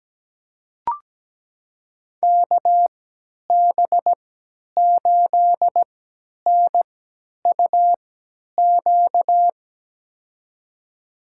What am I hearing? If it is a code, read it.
Morse code, heard as a keyed tone, about 17 wpm, spaced as for 10 wpm, 700 Hz. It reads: KB8NUQ
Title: USA Call Signs - 6 Characters - 3x Character Spacing - 17wpm